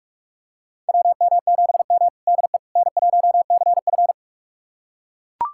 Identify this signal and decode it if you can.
Morse code, heard as a keyed tone, about 43 wpm, 700 Hz. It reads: WM7M DE N1CF
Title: USA Callsign DE Callsign - 4 Characters - 45wpm